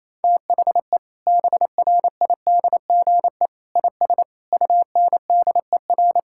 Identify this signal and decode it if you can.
Morse code, heard as a keyed tone, about 28 wpm, 700 Hz. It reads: THE BRIDGE IS UNDER
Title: Sentences from Top 1600 Words 28wpm